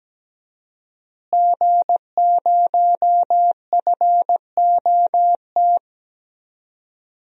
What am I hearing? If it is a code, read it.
Morse code, heard as a keyed tone, about 17 wpm, 700 Hz. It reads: G0FOT
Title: DXCC-100 Call Signs - 5 Characters - Repeated 3x - 17wpm